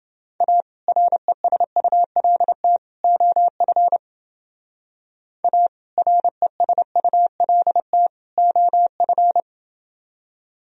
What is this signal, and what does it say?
Morse code, heard as a keyed tone, about 29 wpm, 700 Hz. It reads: A RESULT OF A RESULT OF